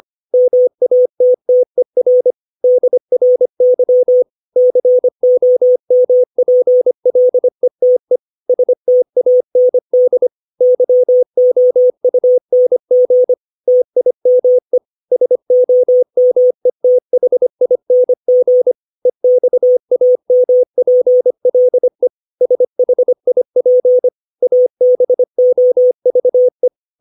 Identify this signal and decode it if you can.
Morse code, heard as a keyed tone, about 25 wpm, 500 Hz. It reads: MATTER DRY COMPLETE STAND YOUNG TIME SOMETHING EXAMPLE SHIP ABOVE
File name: 632 words / 632 Words 25WPM